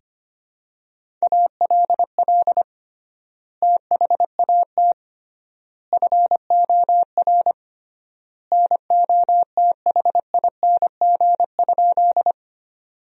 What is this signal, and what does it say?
Morse code, heard as a keyed tone, about 25 wpm, 700 Hz. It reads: ALL THAT FOR NOTHING?
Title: 3x Word-Spacing - Sentences from Top 500 Words - 25wpm